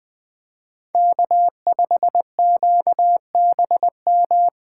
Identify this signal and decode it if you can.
Morse code, heard as a keyed tone, about 20 wpm, 700 Hz. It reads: K5QBM
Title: USA Call Signs - 5 Characters - 2x Character Spacing - 20wpm